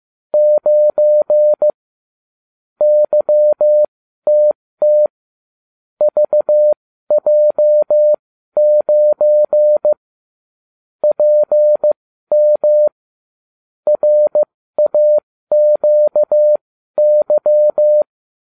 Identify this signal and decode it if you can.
Morse code, heard as a keyed tone, about 15 wpm, spaced as for 12 wpm, 600 Hz. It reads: 9 YTT VJ9 PM RAQY